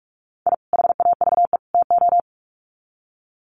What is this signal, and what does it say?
Morse code, heard as a keyed tone, about 44 wpm, 700 Hz. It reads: I HAVE TO
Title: Sets of 3 Words 45wpm